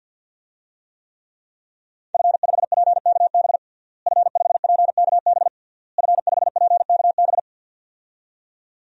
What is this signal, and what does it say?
Morse code, heard as a keyed tone, about 51 wpm, 700 Hz. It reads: F5PKB F5PKB F5PKB
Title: DXCC-100 Call Signs - 5 Characters - Repeated 3x - 50wpm